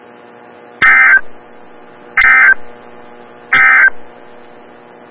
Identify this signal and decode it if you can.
Emergency Alert System (SAME) tones, heard as harsh buzzing bursts. It is heard from 0.7 to 4.0 s.
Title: Des Moines, IA NOAA Weather Radio - WXL57